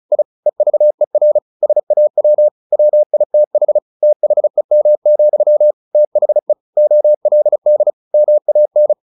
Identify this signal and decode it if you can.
Morse code, heard as a keyed tone, about 34 wpm, 600 Hz. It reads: I EVER SAW WITH THEM, THE OLD MAN
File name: the old man and the sea / the old man and the sea 1 35wpm